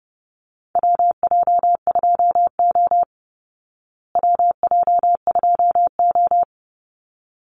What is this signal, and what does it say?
Morse code, heard as a keyed tone, about 30 wpm, 700 Hz. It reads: WJ2O WJ2O
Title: Call Signs Repeated 3x - 30wpm